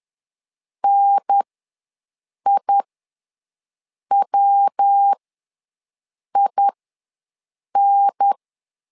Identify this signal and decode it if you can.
Morse code, heard as a keyed tone, about 10 wpm, spaced as for 6 wpm, 800 Hz. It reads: NIWIN
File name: LEZIONI / 3 MWRO / 3 MW ws 10wpm 2213